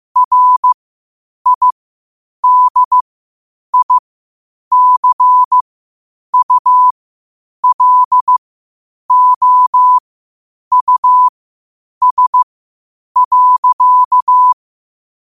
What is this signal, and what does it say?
Morse code, heard as a keyed tone, about 15 wpm, spaced as for 8 wpm, 1000 Hz. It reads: RIDICULOUS.
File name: qotdfiles / 8wpm